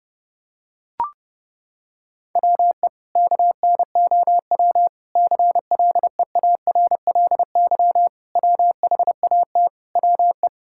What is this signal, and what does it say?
Morse code, heard as a keyed tone, about 32 wpm, 700 Hz. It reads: WE KNOW CLEARLY WHAT WE
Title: Sentences from Top 1600 Words 30wpm